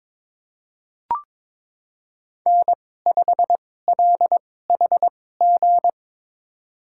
Morse code, heard as a keyed tone, about 21 wpm, spaced as for 16 wpm, 700 Hz. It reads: N5LHG
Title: USA Call Signs - 5 Characters - 2x Character Spacing - 22wpm